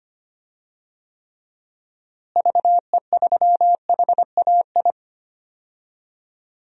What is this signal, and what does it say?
Morse code, heard as a keyed tone, about 25 wpm, 700 Hz. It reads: VE3HAI
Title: DXCC-100 Call Signs - 6 Characters - Repeated 2x - 25wpm